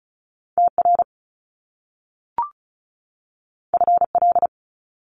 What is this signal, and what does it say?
Morse code, heard as a keyed tone, about 36 wpm, 700 Hz. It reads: TR FL
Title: Top Two Letter Combinations in Words 35wpm